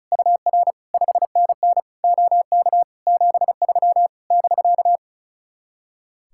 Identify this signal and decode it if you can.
Morse code, heard as a keyed tone, about 36 wpm, 700 Hz. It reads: UR 5NN OK 73 <BK>
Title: Speed-Racing Common QSO Phrases Encore - 35wpm